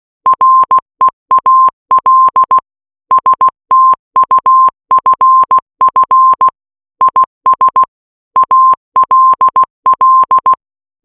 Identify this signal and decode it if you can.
Morse code, heard as a keyed tone, about 16 wpm, 1050 Hz. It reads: REAL STUFF IS ALL